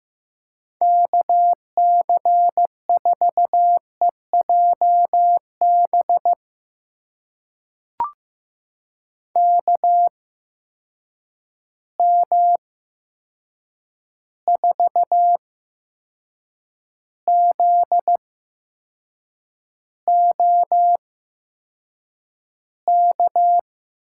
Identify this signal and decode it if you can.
Morse code, heard as a keyed tone, about 15 wpm, spaced as for 6 wpm, 700 Hz. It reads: KC4EJB KM4ZOK
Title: USA Call Signs - 6 Characters - 8x Character Spacing - 15wpm